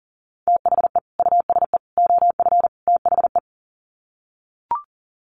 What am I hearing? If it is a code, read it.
Morse code, heard as a keyed tone, about 40 wpm, 700 Hz. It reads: THE USE OF THE
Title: Sets of 4 Words 40wpm